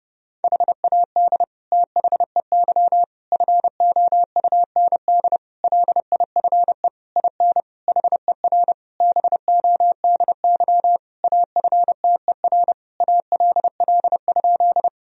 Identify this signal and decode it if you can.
Morse code, heard as a keyed tone, about 30 wpm, 700 Hz. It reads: HAD THEY FOUND LIFE IN HER BODY AFTER ALL?
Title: Sentences from Top 1000 Words - Review Part 2 - 30wpm